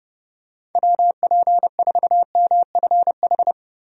Morse code, heard as a keyed tone, about 30 wpm, 700 Hz. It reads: WP4MFH
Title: USA Call Signs - 6 Characters - 30wpm